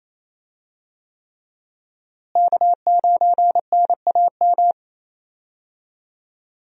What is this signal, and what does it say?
Morse code, heard as a keyed tone, about 28 wpm, 700 Hz. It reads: K9NAM
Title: USA Call Signs - 5 Characters - Repeated 3x - 28wpm